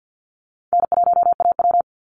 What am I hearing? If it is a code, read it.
Morse code, heard as a keyed tone, about 50 wpm, 700 Hz. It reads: N1AW
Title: NA SOTA Chaser Callsigns 50wpm